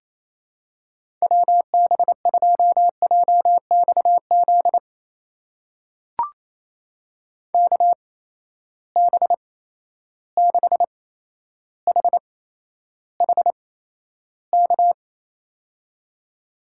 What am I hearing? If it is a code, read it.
Morse code, heard as a keyed tone, about 28 wpm, spaced as for 12 wpm, 700 Hz. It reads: WB2JXZ KB6HHK
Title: USA Call Signs - 6 Characters - 8x Character Spacing - 28wpm